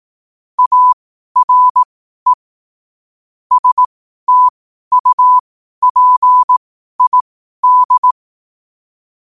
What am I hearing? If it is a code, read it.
Morse code, heard as a keyed tone, about 18 wpm, spaced as for 12 wpm, 1000 Hz. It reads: ARE STUPID
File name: qotdfiles / 12wpm